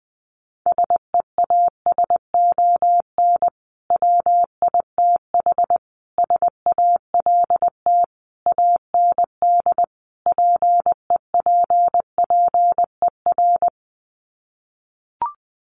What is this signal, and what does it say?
Morse code, heard as a keyed tone, about 20 wpm, 700 Hz. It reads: SEASON WITH SALT AND PEPPER
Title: Sets of 5 Words 20wpm